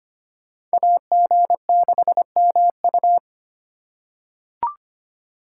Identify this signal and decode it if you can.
Morse code, heard as a keyed tone, about 25 wpm, 700 Hz. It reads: AG6MU
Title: USA Call Signs - 5 Characters - 3x Character Spacing - 25wpm